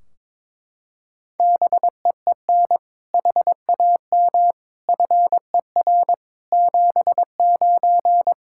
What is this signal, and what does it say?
Morse code, heard as a keyed tone, about 22 wpm, 700 Hz. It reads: BEEN HAM FER 79
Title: Practice QSO - Exchange 3 - More Info - 22wpm